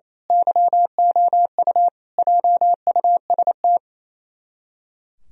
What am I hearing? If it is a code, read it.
Morse code, heard as a keyed tone, about 28 wpm, 700 Hz. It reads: YOU JUST
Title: Speed-Racing - Sets of 5 Words 28wpm